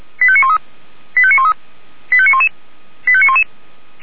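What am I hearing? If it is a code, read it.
Five-tone selective call (ZVEI), heard as quick runs of five tones, tones of about 70 ms. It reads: 85913 85913 85911 85911
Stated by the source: ZVEI five-tone selective call received on an SDR.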